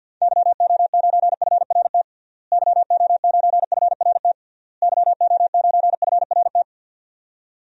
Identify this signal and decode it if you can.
Morse code, heard as a keyed tone, about 51 wpm, 700 Hz. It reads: YO9FRT YO9FRT YO9FRT
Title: DXCC-100 Call Signs - 6 Characters - Repeated 3x - 50wpm